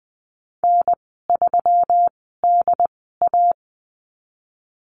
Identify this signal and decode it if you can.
Morse code, heard as a keyed tone, about 20 wpm, spaced as for 14 wpm, 700 Hz. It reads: N3DA
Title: 140 Lesson 23g - Letter f - Call Signs 20wpm